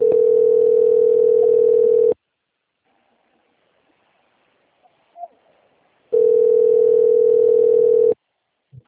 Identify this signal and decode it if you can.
Ringback tone (North America), 440 + 480 Hz.